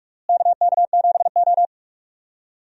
Morse code, heard as a keyed tone, about 42 wpm, 700 Hz. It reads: KK7O
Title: USA Call Signs - 4 Characters - Repeated 3x - 45wpm